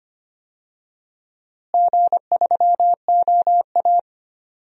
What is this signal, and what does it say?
Morse code, heard as a keyed tone, about 25 wpm, 700 Hz. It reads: G3OA
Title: DXCC-100 Call Signs - 4 Characters - Repeated 3x - 25wpm